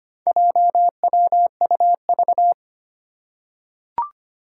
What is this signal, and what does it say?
Morse code, heard as a keyed tone, about 25 wpm, 700 Hz. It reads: JWUV
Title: Four Challenging Letter-Number Combinations 25wpm